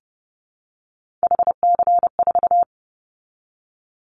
Morse code, heard as a keyed tone, about 30 wpm, 700 Hz. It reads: HC4
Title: Callsign Prefixes for Top 100 DXCC - 30wpm